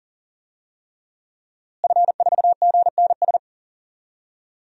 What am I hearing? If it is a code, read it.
Morse code, heard as a keyed tone, about 37 wpm, 700 Hz. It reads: F4GNS
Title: DXCC-100 Call Signs - 5 Characters - Repeated 3x - 40wpm